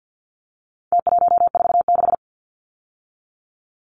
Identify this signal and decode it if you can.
Morse code, heard as a keyed tone, about 49 wpm, 700 Hz. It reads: T146
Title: Numbers - Contest Style - 4-Digit 50wpm